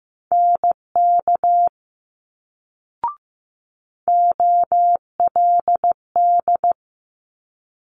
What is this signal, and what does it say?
Morse code, heard as a keyed tone, about 15 wpm, 700 Hz. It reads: NK OLD
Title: Multi-letter Phonemes Quick Fire 15wpm